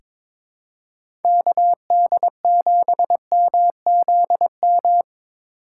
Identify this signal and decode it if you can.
Morse code, heard as a keyed tone, about 22 wpm, 700 Hz. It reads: KD7MZM
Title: USA Call Signs - 6 Characters - 3x Character Spacing - 22wpm